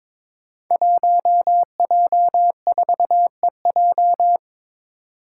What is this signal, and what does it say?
Morse code, heard as a keyed tone, about 22 wpm, 700 Hz. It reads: 1J4EJ
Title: Five Challenging Letter-Number Combinations 22wpm